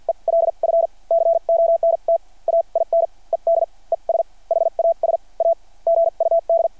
Morse code, heard as a keyed tone, about 49 wpm, spaced as for 36 wpm, 600 Hz. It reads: ELF CONTAINED ES HAS A GUD
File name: archived / CRSnet